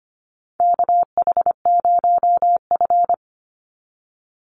Morse code, heard as a keyed tone, about 25 wpm, 700 Hz. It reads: KH0F
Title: USA Call Signs 25wpm